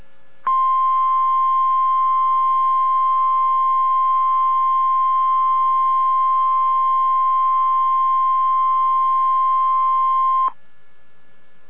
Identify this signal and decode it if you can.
Emergency Alert System (SAME) tones (the 1050 Hz attention tone). It is heard from 0.4 to 10.5 s.